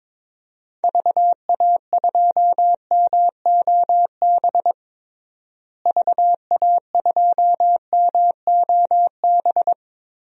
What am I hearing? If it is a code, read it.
Morse code, heard as a keyed tone, about 21 wpm, 700 Hz. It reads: VA2MOB VA2MOB
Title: DXCC-100 Call Signs - 6 Characters - Repeated 3x - 22wpm